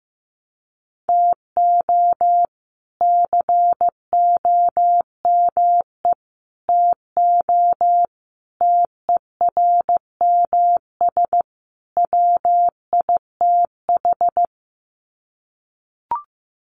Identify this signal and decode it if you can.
Morse code, heard as a keyed tone, about 15 wpm, 700 Hz. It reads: TO COME TO TERMS WITH
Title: Sets of 5 Words Encore 15wpm